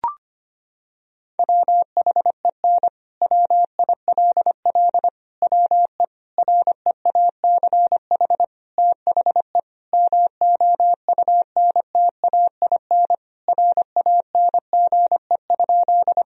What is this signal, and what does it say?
Morse code, heard as a keyed tone, about 25 wpm, 700 Hz. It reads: WHEN WILL WE REACH THE MOUNTAIN RANGE?
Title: Sentences from Top 1000 Words - Review Part 4 - 25wpm